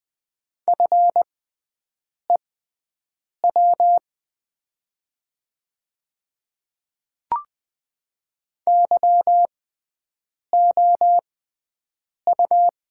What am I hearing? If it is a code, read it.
Morse code, heard as a keyed tone, about 20 wpm, spaced as for 7 wpm, 700 Hz. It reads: FEW YOU
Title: 6x Character Spacing - Top 300 Words - 20wpm